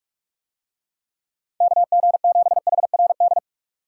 Morse code, heard as a keyed tone, about 46 wpm, 700 Hz. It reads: KG7HRD